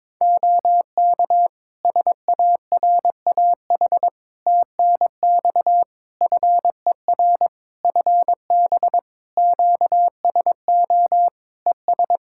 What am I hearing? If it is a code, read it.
Morse code, heard as a keyed tone, about 22 wpm, 700 Hz. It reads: OK SARAH TNX FER FB QSO ES
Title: Practice QSO - Exchange 4 - Ending - 22wpm